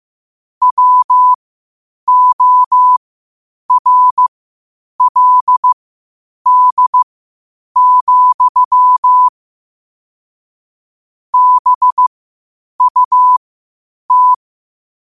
Morse code, heard as a keyed tone, about 15 wpm, spaced as for 8 wpm, 1000 Hz. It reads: WORLD, BUT
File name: qotdfiles / 8wpm